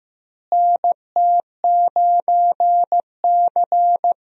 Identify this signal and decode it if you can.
Morse code, heard as a keyed tone, about 15 wpm, 700 Hz. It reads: NT9C
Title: USA Call Signs - 4 Characters - Repeated 2x - 15wpm